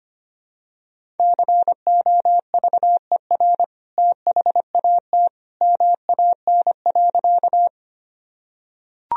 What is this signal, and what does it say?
Morse code, heard as a keyed tone, about 25 wpm, 700 Hz. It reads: COVER THAT MAN.
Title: Sentences from Top 1000 Words - Review Part 1 - 25wpm